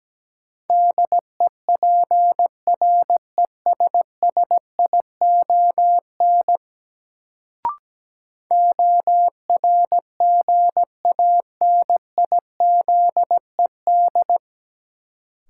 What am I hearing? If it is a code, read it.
Morse code, heard as a keyed tone, about 17 wpm, 700 Hz. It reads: DEPRESSION ORGANIZED
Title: Sentences from Top 1300 Words - New Word Drill Alt - 17wpm